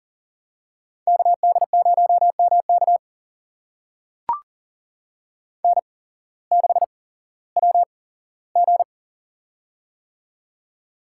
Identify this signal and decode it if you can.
Morse code, heard as a keyed tone, about 37 wpm, spaced as for 18 wpm, 700 Hz. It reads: KD0MK N6WG
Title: USA Call Signs - 8x Character Spacing - 40wpm